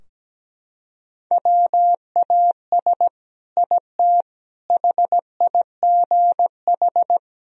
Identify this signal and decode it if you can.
Morse code, heard as a keyed tone, about 17 wpm, 700 Hz. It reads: WAS IT HIGH
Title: Sentences from Top 1000 Words - Review Part 4 - 17wpm